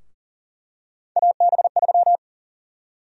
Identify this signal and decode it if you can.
Morse code, heard as a keyed tone, about 40 wpm, 700 Hz. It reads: AB3
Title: USA Call Sign Prefixes - 40wpm